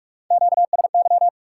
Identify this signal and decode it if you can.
Morse code, heard as a keyed tone, about 44 wpm, 700 Hz. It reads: QSY
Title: Top 180 QSO Elements 45wpm